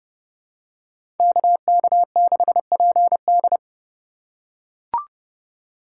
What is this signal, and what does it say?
Morse code, heard as a keyed tone, about 30 wpm, 700 Hz. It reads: KK6PD
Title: USA Call Signs - 5 Characters - 4x Character Spacing - 30wpm